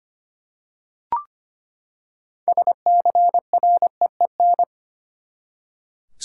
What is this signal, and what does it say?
Morse code, heard as a keyed tone, about 25 wpm, 700 Hz. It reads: SCREEN
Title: Six Letter Words - 25wpm